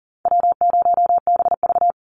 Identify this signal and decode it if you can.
Morse code, heard as a keyed tone, about 40 wpm, 700 Hz. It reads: W0BV
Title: Easy Call Signs Repeated 3x - 40wpm